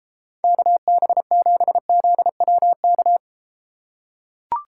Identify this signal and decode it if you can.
Morse code, heard as a keyed tone, about 33 wpm, 700 Hz. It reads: KB7ZWK